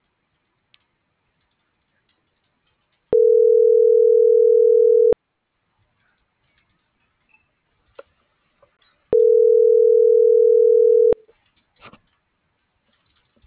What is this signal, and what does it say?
Ringback tone (North America), 440 + 480 Hz.
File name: CENTRAL SOTOVA / out alejandro.rodriguez